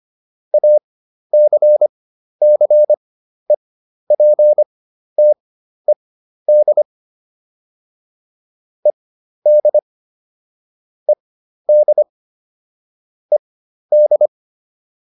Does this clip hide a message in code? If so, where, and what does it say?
Yes: Morse code from 0.5 to 14.3 s, heard as a keyed tone, about 25 wpm, spaced as for 12 wpm, 600 Hz. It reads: ACCEPTED ED ED ED